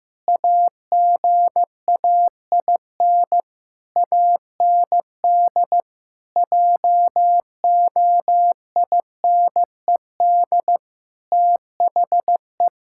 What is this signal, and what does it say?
Morse code, heard as a keyed tone, about 15 wpm, 700 Hz. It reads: AGAIN AND JOINED THE
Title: Pangram Challenge v2 - 15wpm